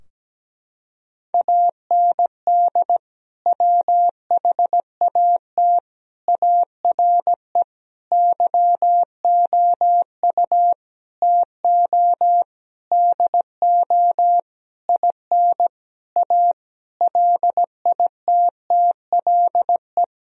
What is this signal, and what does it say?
Morse code, heard as a keyed tone, about 17 wpm, 700 Hz. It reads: AND WHAT ARE YOU TO DO IN A LITTLE
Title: Sentences from Top 1000 Words - Review Part 2 - 17wpm